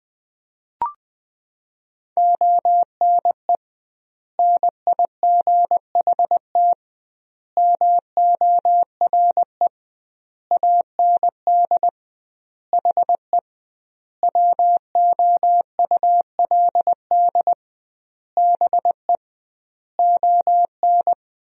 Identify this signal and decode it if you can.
Morse code, heard as a keyed tone, about 21 wpm, 700 Hz. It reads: ONE NIGHT MORE AND HE WOULD BE ON
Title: 2x Word-Spacing - Sentences from Top 300 Words - 20wpm